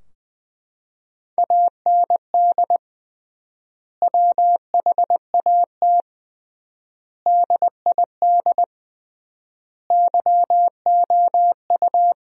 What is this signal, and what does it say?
Morse code, heard as a keyed tone, about 20 wpm, 700 Hz. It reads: AND WHAT DID YOU
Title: 3x Word-Spacing - Sentences from Top 100 Words - 20wpm